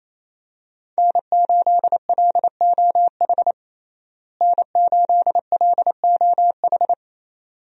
Morse code, heard as a keyed tone, about 28 wpm, 700 Hz. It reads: N8LOH N8LOH
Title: USA Call Signs - 5 Characters - Repeated 2x - 28wpm